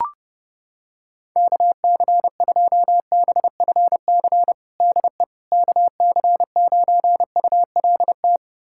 Morse code, heard as a keyed tone, about 30 wpm, 700 Hz. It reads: KC2BFC DE KC9ULT
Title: USA Callsign DE Callsign - 6 Characters - 30wpm